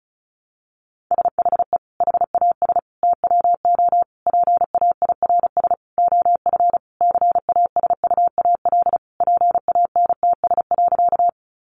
Morse code, heard as a keyed tone, about 34 wpm, 700 Hz. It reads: SHE HAS TWO PAIRS OF CASUAL PANTS.
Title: Additional Easy Phrases 35wpm